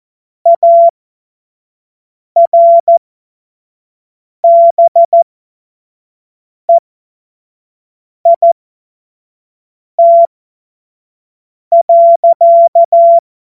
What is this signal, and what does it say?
Morse code, heard as a keyed tone, about 14 wpm, spaced as for 5 wpm, 680 Hz. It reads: ARBEIT.
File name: Morsekurs / Witze / Witz 27. Dezember 2022 25 F70